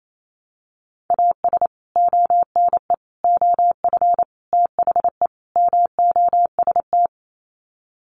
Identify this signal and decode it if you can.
Morse code, heard as a keyed tone, about 28 wpm, 700 Hz. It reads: AS ONE OF THE MOST
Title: Sets of 5 Words 28wpm